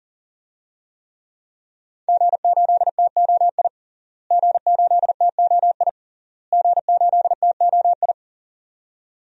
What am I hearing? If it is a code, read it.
Morse code, heard as a keyed tone, about 40 wpm, 700 Hz. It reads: G8TOI G8TOI G8TOI
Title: DXCC-100 Call Signs - 5 Characters - Repeated 3x - 40wpm